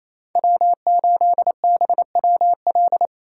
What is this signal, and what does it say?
Morse code, heard as a keyed tone, about 34 wpm, 700 Hz. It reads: W8BWL
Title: Speed Racing - USA 1x3 Call Signs 28wpm